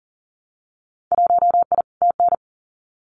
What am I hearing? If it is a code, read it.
Morse code, heard as a keyed tone, about 38 wpm, 700 Hz. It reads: 1I TN
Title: Winter Field Day Exchanges Encore 40wpm